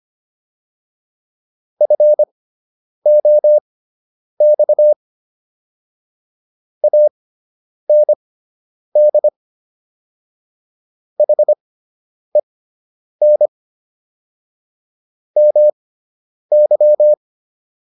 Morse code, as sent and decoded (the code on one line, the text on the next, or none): ..-. --- -..- / .- -. -.. / .... -. / -- -.--
FOX AND HEN MY